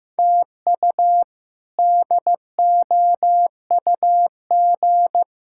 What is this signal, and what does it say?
Morse code, heard as a keyed tone, about 15 wpm, 700 Hz. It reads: TU DOUG
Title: K1USN SST QSO Practice - 15wpm